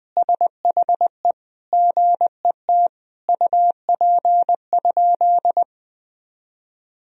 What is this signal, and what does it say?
Morse code, heard as a keyed tone, about 20 wpm, 700 Hz. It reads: SHE GET UP?
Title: Sentences from Top 1000 Words - Review Part 1 - 20wpm